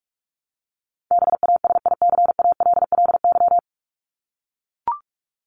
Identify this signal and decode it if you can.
Morse code, heard as a keyed tone, about 44 wpm, 700 Hz. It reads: BASICALLY
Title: Top 1251-1500 Words 45wpm